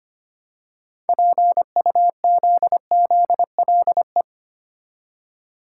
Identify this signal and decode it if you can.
Morse code, heard as a keyed tone, about 25 wpm, 700 Hz. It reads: PUZZLE
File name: practice / 25wpm / Easy Words with Hints - 25wpm